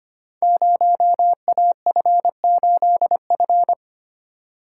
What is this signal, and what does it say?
Morse code, heard as a keyed tone, about 25 wpm, 700 Hz. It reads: 0AF8F
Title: Code Groups - 5 Letter:Numbers - 25wpm